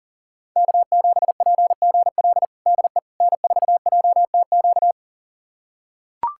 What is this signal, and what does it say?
Morse code, heard as a keyed tone, about 40 wpm, 700 Hz. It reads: K7PGL DE N4JTQ
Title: USA Callsign DE Callsign - 5 Characters - 40wpm